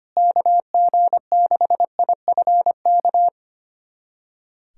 Morse code, heard as a keyed tone, about 25 wpm, 700 Hz. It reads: KG6IFK
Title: USA Callsign DE Callsign 25wpm